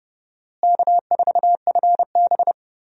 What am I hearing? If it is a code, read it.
Morse code, heard as a keyed tone, about 30 wpm, 700 Hz. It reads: K4FB
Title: USA Call Signs - 4 Characters - Repeated 3x - 30wpm